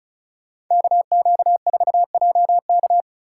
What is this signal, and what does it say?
Morse code, heard as a keyed tone, about 44 wpm, 700 Hz. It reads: KQ4JK